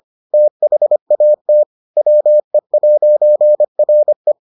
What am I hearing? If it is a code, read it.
Morse code, heard as a keyed tone, about 25 wpm, 600 Hz. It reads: THAT WE'RE